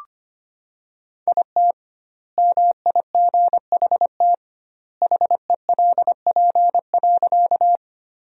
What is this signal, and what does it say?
Morse code, heard as a keyed tone, about 25 wpm, 700 Hz. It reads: IT MIGHT HELP.
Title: 2x Word-Spacing - Sentences from Top 300 Words - 25wpm